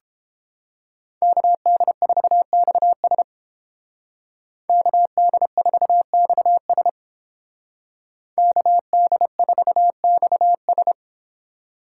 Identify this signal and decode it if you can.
Morse code, heard as a keyed tone, about 31 wpm, 700 Hz. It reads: KD4XS KD4XS KD4XS